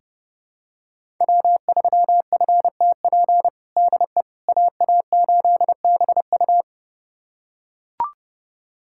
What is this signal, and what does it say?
Morse code, heard as a keyed tone, about 32 wpm, 700 Hz. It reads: W3FTP DE AA8BU
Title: USA Callsign DE Callsign - 5 Characters - 30wpm